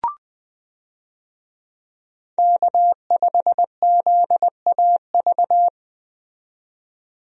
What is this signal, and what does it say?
Morse code, heard as a keyed tone, about 20 wpm, 700 Hz. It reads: K5ZAV
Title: USA Call Signs - 5 Characters - 20wpm